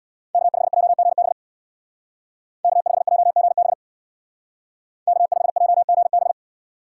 Morse code, heard as a keyed tone, about 61 wpm, 700 Hz. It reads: K5JGB K5JGB K5JGB
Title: Speed Racing - USA 1x3 Call Signs 50wpm